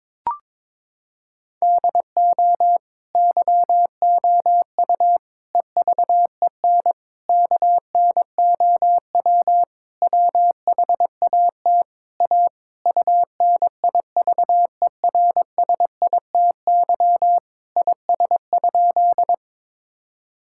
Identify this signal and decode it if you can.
Morse code, heard as a keyed tone, about 22 wpm, 700 Hz. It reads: DO YOU EVEN KNOW WHAT A UNIVERSITY IS?
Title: Sentences from Top 1000 Words - Review Part 5 - 22wpm